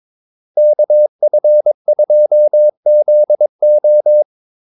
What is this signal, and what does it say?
Morse code, heard as a keyed tone, about 22 wpm, 600 Hz. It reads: KF2ZO
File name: Callsigns-22wpm